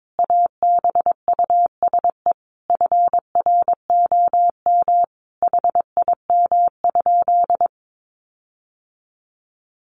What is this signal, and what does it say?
Morse code, heard as a keyed tone, about 22 wpm, 700 Hz. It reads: ABUSE FROM HIM?
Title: Sentences from Top 1300 Words 22wpm